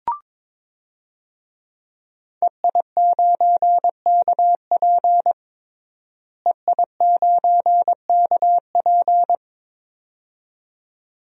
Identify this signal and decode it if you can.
Morse code, heard as a keyed tone, about 22 wpm, 700 Hz. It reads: EI9KP EI9KP
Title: DXCC-100 Call Signs - 5 Characters - Repeated 2x - 22wpm